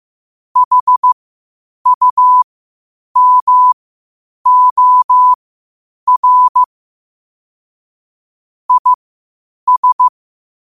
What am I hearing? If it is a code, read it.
Morse code, heard as a keyed tone, about 15 wpm, spaced as for 8 wpm, 1000 Hz. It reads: HUMOR IS
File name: qotdfiles / 8wpm